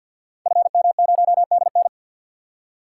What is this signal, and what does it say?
Morse code, heard as a keyed tone, about 52 wpm, 700 Hz. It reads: FM0DN
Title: Code Groups - 5 Letter:Numbers - 50wpm